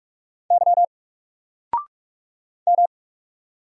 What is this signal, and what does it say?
Morse code, heard as a keyed tone, about 44 wpm, 700 Hz. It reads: Y M
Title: Single Character (Letter:Number:Punctuation:Prosign) Alt 45wpm